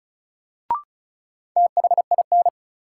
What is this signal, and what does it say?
Morse code, heard as a keyed tone, about 36 wpm, 700 Hz. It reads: THIN
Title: Sentences from Top 1200 Words - New Word Drill - 35wpm